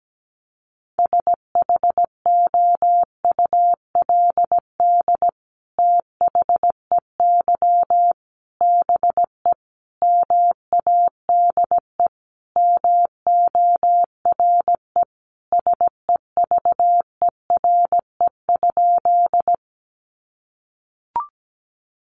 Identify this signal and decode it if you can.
Morse code, heard as a keyed tone, about 17 wpm, 700 Hz. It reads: SHOULD THEY BE MADE MORE SEVERE?
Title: Sentences from Top 1400 Words 17wpm